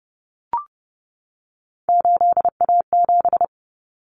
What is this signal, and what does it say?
Morse code, heard as a keyed tone, about 30 wpm, 700 Hz. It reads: 8A7
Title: Common Cut Numbers - 3-Digit 30wpm